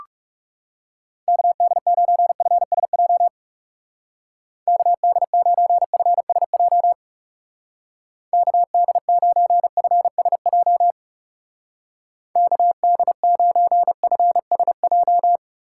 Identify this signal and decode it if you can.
Morse code, heard as a keyed tone, about 36 wpm, 700 Hz. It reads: KD9FSJ KD9FSJ KD9FSJ KD9FSJ
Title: Speed Racing - USA 2x3 Call Signs 30wpm